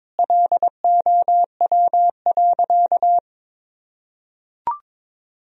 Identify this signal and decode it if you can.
Morse code, heard as a keyed tone, about 22 wpm, 700 Hz. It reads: LOW.
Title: Sentences from Top 1000 Words - Review Part 1 - 22wpm